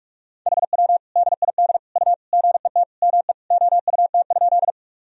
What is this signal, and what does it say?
Morse code, heard as a keyed tone, about 41 wpm, 700 Hz. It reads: HW DID U GET ME OUT?
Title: Sentences from Top 200 Words and Abbreviations 45wpm